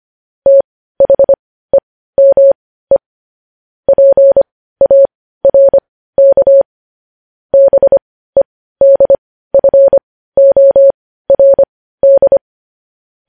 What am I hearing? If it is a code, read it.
Morse code, heard as a keyed tone, about 25 wpm, spaced as for 15 wpm, 550 Hz. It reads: THEME PARK BEDFORD
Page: news headlines in morse code at 15 wpm